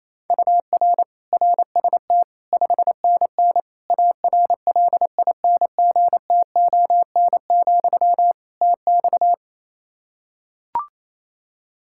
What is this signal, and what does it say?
Morse code, heard as a keyed tone, about 28 wpm, 700 Hz. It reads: UR RST 5NN ARLINGTON, TX
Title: Practice QSO - Exchange 1 - Beginning - 28wpm